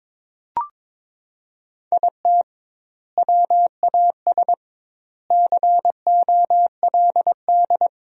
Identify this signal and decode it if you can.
Morse code, heard as a keyed tone, about 22 wpm, 700 Hz. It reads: IT WAS COLD
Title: 2x Word-Spacing - Sentences from Top 500 Words - 22wpm